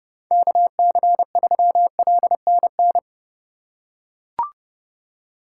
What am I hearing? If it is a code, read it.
Morse code, heard as a keyed tone, about 30 wpm, 700 Hz. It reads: KC3LNN
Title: USA Call Signs - 2x Character Spacing - 30wpm